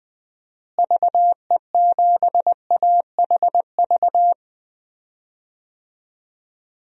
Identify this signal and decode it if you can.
Morse code, heard as a keyed tone, about 20 wpm, 700 Hz. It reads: VE7AHV
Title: DXCC-100 Call Signs - 6 Characters - Repeated 2x - 20wpm